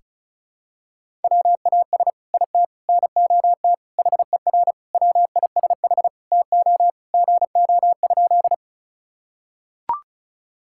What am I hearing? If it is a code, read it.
Morse code, heard as a keyed tone, about 34 wpm, 700 Hz. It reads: WAS IT NOT HER WISH TO GO?
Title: Sentences from Top 1600 Words 35wpm